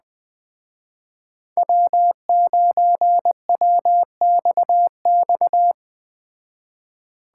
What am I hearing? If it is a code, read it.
Morse code, heard as a keyed tone, about 20 wpm, 700 Hz. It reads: W9WXX
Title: Speed Racing - USA 1x3 Call Signs 17wpm